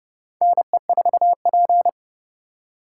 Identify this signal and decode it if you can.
Morse code, heard as a keyed tone, about 30 wpm, 700 Hz. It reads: NE4P
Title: USA Call Signs - 3x Character Spacing - 30wpm